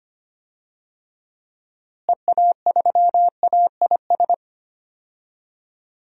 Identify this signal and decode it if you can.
Morse code, heard as a keyed tone, about 25 wpm, 700 Hz. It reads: EA3AIS
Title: DXCC-100 Call Signs - 6 Characters - Repeated 2x - 25wpm